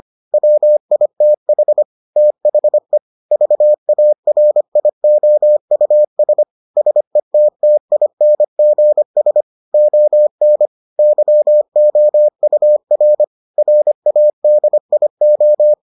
Morse code, heard as a keyed tone, about 25 wpm, 600 Hz. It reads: WITH THE VARIOUS SETTINGS ON YOUR RADIO